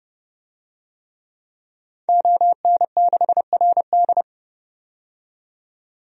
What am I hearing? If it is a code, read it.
Morse code, heard as a keyed tone, about 29 wpm, 700 Hz. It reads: ON6RD
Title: DXCC-100 Call Signs - 5 Characters - 30wpm